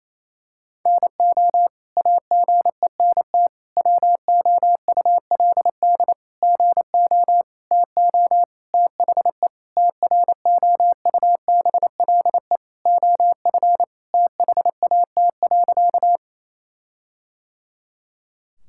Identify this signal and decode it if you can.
Morse code, heard as a keyed tone, about 28 wpm, 700 Hz. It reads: NO AGENT WOULD GO TO THE TROUBLE OF THAT.
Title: Sentences from Top 1600 Words 28wpm